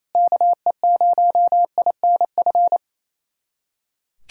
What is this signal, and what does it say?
Morse code, heard as a keyed tone, about 28 wpm, 700 Hz. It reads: KE0INF
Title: USA Callsign DE Callsign 28wpm